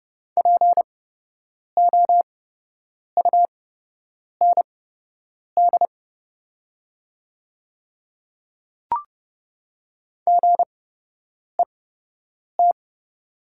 Morse code, heard as a keyed tone, about 30 wpm, spaced as for 8 wpm, 700 Hz. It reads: POUND GET
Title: 8x Character Spacing - Top 500 Words - 30wpm